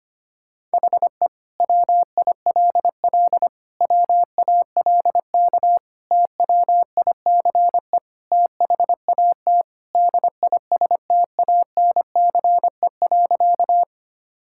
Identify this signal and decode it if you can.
Morse code, heard as a keyed tone, about 25 wpm, 700 Hz. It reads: HE WILL WALK TWICE THAT DISTANCE.
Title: Sentences from Top 1600 Words 25wpm